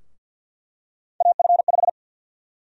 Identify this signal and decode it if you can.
Morse code, heard as a keyed tone, about 53 wpm, 700 Hz. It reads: AF5
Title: USA Call Sign Prefixes - 50wpm